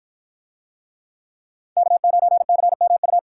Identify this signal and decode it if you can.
Morse code, heard as a keyed tone, about 53 wpm, 700 Hz. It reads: K9CMU